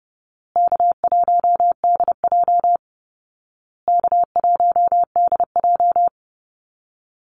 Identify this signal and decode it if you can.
Morse code, heard as a keyed tone, about 28 wpm, 700 Hz. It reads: K1DJ K1DJ
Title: Easy Call Signs Repeated 3x - 30wpm